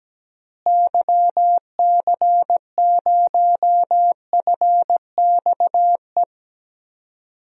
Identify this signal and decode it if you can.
Morse code, heard as a keyed tone, about 17 wpm, 700 Hz. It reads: YC0FXE
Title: DXCC-100 Call Signs - 6 Characters - Repeated 2x - 17wpm